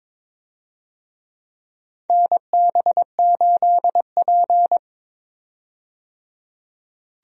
Morse code, heard as a keyed tone, about 22 wpm, 700 Hz. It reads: NB8P
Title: USA Call Signs - 4 Characters - 22wpm